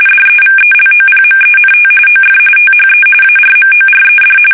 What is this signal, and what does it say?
RTTY, about 45.45 baud. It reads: DEVELOPMENT OF THE ARMORED
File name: fsk850 / battleshipnj